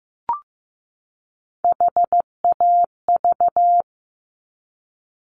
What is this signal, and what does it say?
Morse code, heard as a keyed tone, about 15 wpm, 700 Hz. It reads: HAV
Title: Top Three Letter Combinations in Words 15wpm